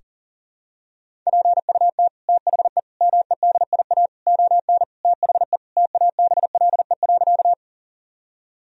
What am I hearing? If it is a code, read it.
Morse code, heard as a keyed tone, about 40 wpm, 700 Hz. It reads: PUT THE MEDIA ON THE TABLE.
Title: Sentences from Top 1000 Words - Review Part 4 - 40wpm